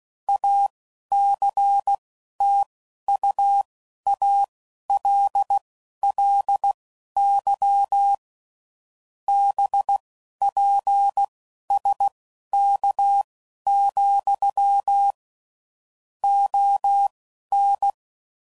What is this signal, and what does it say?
Morse code, heard as a keyed tone, about 16 wpm, spaced as for 11 wpm, 800 Hz. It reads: ACTUALLY BPSK, ON
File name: CW Code Groups @10WPM / QSOs / QSO kc7sac